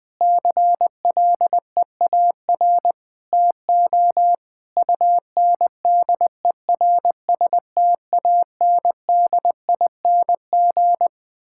Morse code, heard as a keyed tone, about 20 wpm, 700 Hz. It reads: CLEAR TO UNDERSTANDING
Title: Sentences with 9-Word Phrases - Spoken Beginning - 20wpm